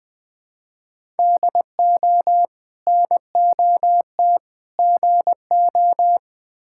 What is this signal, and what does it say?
Morse code, heard as a keyed tone, about 21 wpm, 700 Hz. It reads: DO NOT GO
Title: Build A Phrase - 4-Word Phrases - 20wpm